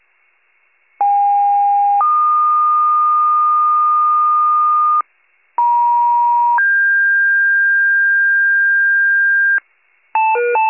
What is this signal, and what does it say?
Two-tone page, heard as a one-second tone, then a three-second tone, about 810 then 1220 Hz, about 950 then 1600 Hz, with noise. It is heard from 1.0 to 9.6 s.
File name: 83 RESCUE DUTY